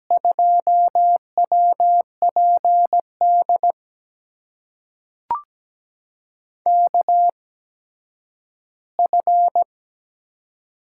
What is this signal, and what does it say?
Morse code, heard as a keyed tone, about 17 wpm, spaced as for 10 wpm, 700 Hz. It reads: 2WPD KF
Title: USA Call Signs - 6 Characters - 8x Character Spacing - 17wpm